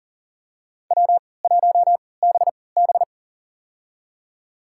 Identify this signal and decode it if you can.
Morse code, heard as a keyed tone, about 40 wpm, spaced as for 23 wpm, 700 Hz. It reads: W1BB
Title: USA Call Signs - 4 Characters - 3x Character Spacing - 40wpm